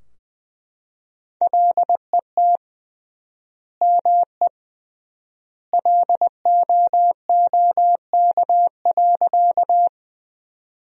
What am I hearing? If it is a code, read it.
Morse code, heard as a keyed tone, about 20 wpm, 700 Hz. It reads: LET ME LOOK.
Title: 3x Word-Spacing - Sentences from Top 300 Words - 20wpm